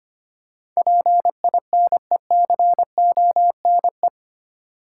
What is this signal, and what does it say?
Morse code, heard as a keyed tone, about 25 wpm, 700 Hz. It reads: PINECONE
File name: practice / 25wpm / Difficult Words with Hints - 25wpm